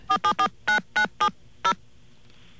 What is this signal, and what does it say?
DTMF, heard as telephone touch tones, tones of about 90 ms. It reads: 4749970